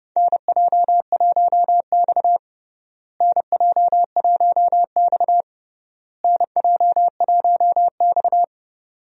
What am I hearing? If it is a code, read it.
Morse code, heard as a keyed tone, about 30 wpm, 700 Hz. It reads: NJ1X NJ1X NJ1X
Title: USA Call Signs - 4 Characters - Repeated 3x - 30wpm